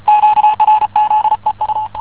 Morse code, heard as a keyed tone, about 34 wpm, 850 Hz, with radio noise. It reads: QRZ EU
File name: w9wnv-zk1s-14cw